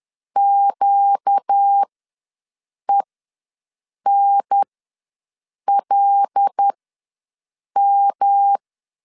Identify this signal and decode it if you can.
Morse code, heard as a keyed tone, about 10 wpm, spaced as for 6 wpm, 800 Hz. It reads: QENLM
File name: LEZIONI / 6 PKX0 / 6 P ws 10wpm 3751